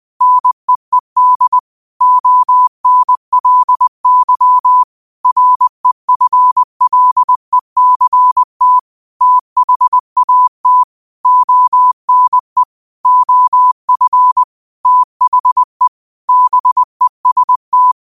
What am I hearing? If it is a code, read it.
Morse code, heard as a keyed tone, about 20 wpm, 1000 Hz. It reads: NEED ONLY REFLECT THAT ONE OF THE BEST
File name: qotdfiles / 20wpm